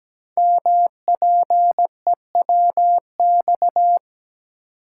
Morse code, heard as a keyed tone, about 17 wpm, 700 Hz. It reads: MPEWX
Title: Code Groups - 5 Letters - 17wpm